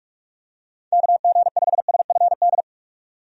Morse code, heard as a keyed tone, about 46 wpm, 700 Hz. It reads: KG5SFD
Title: USA Call Signs - 6 Characters - 45wpm